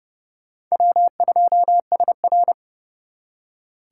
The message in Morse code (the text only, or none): W2SR